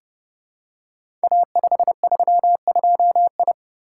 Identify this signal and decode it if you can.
Morse code, heard as a keyed tone, about 30 wpm, 700 Hz. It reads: A532I
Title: Code Groups - 5 Letter:Numbers - 30wpm